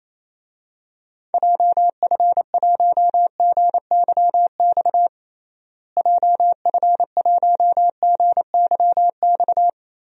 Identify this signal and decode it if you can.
Morse code, heard as a keyed tone, about 28 wpm, 700 Hz. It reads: JF1GYX JF1GYX
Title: DXCC-100 Call Signs - 6 Characters - Repeated 3x - 28wpm